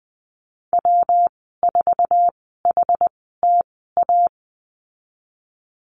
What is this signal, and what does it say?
Morse code, heard as a keyed tone, about 20 wpm, spaced as for 15 wpm, 700 Hz. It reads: W4HTA
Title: 173 Lesson 28g - Number 9 - Call Signs 20wpm